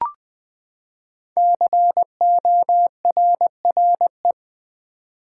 Morse code, heard as a keyed tone, about 20 wpm, 700 Hz. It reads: CORRE
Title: Build A Word - 5 Letters At A Time - 10-Letter Words - 20wpm